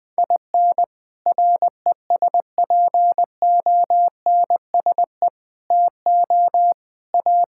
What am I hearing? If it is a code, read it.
Morse code, heard as a keyed tone, about 21 wpm, 700 Hz. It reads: IN RESPONSE TO A
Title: Sentences with 6-Word Phrases - Spoken Beginning - 20wpm